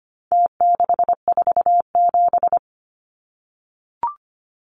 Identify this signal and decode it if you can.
Morse code, heard as a keyed tone, about 25 wpm, 700 Hz. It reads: T647
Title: Numbers - Contest Style - 4-Digit 25wpm